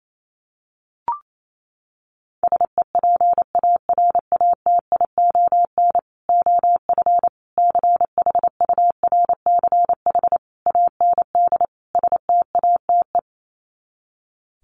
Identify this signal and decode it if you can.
Morse code, heard as a keyed tone, about 28 wpm, 700 Hz. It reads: SEPARATION OF CHURCH AND STATE
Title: Sets of 5 Words 28wpm